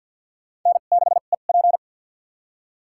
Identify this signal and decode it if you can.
Morse code, heard as a keyed tone, about 53 wpm, spaced as for 37 wpm, 690 Hz. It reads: N6EP
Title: USA Call Signs - 2x Character Spacing - 50wpm